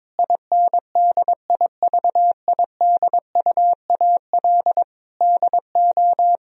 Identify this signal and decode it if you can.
Morse code, heard as a keyed tone, about 22 wpm, 700 Hz. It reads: INDIVIDUAL DO
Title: Sentences from Top 1000 Words - Review Part 4 - 22wpm